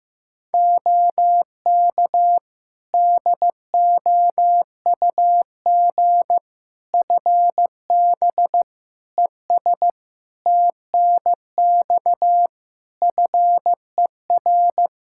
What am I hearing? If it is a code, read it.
Morse code, heard as a keyed tone, about 15 wpm, 700 Hz. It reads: OK DOUG FB ES TNX FER
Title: Practice QSO - Exchange 2 - Info - 15wpm